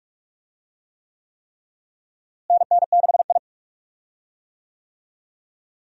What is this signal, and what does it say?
Morse code, heard as a keyed tone, about 46 wpm, 700 Hz. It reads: NN6I
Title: USA Call Signs - 4 Characters - Repeated 3x - 45wpm